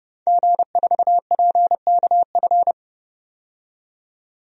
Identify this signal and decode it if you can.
Morse code, heard as a keyed tone, about 30 wpm, 700 Hz. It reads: G4PKF
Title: DXCC-100 Call Signs - 5 Characters - 30wpm